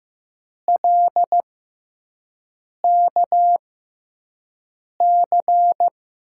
Morse code, heard as a keyed tone, about 15 wpm, spaced as for 5 wpm, 700 Hz. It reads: LKC
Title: USA Call Signs - 6 Characters - 6x Character Spacing - 15wpm